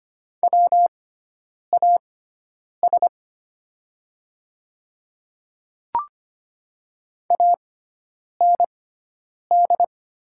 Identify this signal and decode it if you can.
Morse code, heard as a keyed tone, about 25 wpm, spaced as for 9 wpm, 700 Hz. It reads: WAS AND
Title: 6x Character Spacing - Top 100 Words - 25wpm